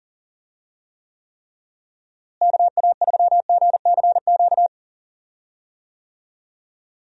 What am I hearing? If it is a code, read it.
Morse code, heard as a keyed tone, about 40 wpm, 700 Hz. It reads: KA3GCQ
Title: USA Call Signs - 6 Characters - Repeated 3x - 40wpm